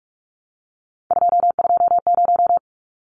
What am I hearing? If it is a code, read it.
Morse code, heard as a keyed tone, about 44 wpm, 700 Hz. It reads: 220
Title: Numbers - 3-Digit 45wpm